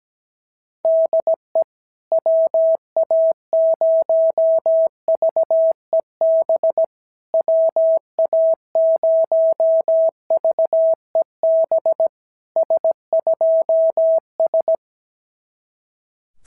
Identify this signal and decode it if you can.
Morse code, heard as a keyed tone, about 17 wpm, 650 Hz. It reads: DE WA0VEB WA0VEB S2S
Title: SOTA QSO Practice 17wpm